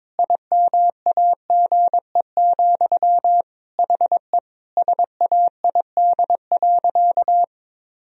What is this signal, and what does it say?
Morse code, heard as a keyed tone, about 22 wpm, 700 Hz. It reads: IMAGE, HE SAID.
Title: Sentences from Top 1000 Words - Review Part 4 - 22wpm